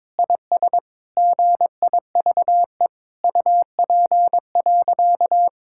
Morse code, heard as a keyed tone, about 22 wpm, 700 Hz. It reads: IS GIVE UP.
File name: Sentences with 7-Word Phrases - 22wpm